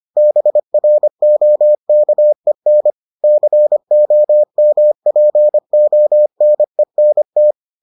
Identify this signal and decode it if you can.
Morse code, heard as a keyed tone, about 25 wpm, 600 Hz. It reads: BROKEN COMPONENT